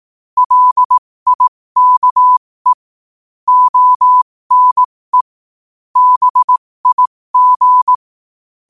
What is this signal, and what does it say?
Morse code, heard as a keyed tone, about 18 wpm, spaced as for 15 wpm, 1000 Hz. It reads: LIKE ONE BIG
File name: qotdfiles / 15wpm